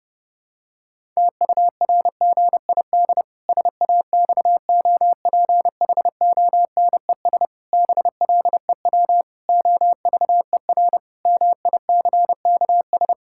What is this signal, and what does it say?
Morse code, heard as a keyed tone, about 32 wpm, 700 Hz. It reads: TURGID SAXOPHONES BLEW OVER MICKS
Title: Pangram Challenge v2 - 30wpm